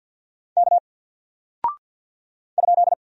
Morse code, heard as a keyed tone, about 50 wpm, 700 Hz. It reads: K ?
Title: Single Character (Letter:Number:Punctuation:Prosign) 50wpm